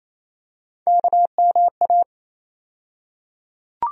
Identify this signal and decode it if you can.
Morse code, heard as a keyed tone, about 28 wpm, 700 Hz. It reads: KMA
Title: USA Call Sign Postfixes - 3 Letters 28wpm